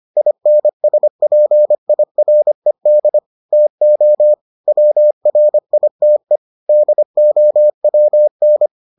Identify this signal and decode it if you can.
Morse code, heard as a keyed tone, about 25 wpm, 600 Hz. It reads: INSPIRED TO WRITE DOWN